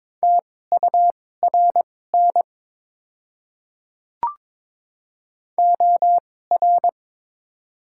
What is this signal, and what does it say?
Morse code, heard as a keyed tone, about 22 wpm, spaced as for 16 wpm, 700 Hz. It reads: TURN OR
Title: 2x Character Spacing - Top 200 Words - 22wpm